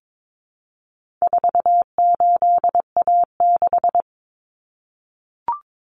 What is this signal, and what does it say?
Morse code, heard as a keyed tone, about 22 wpm, 700 Hz. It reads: 48A6
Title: Common Cut Numbers - 4-Digit 22wpm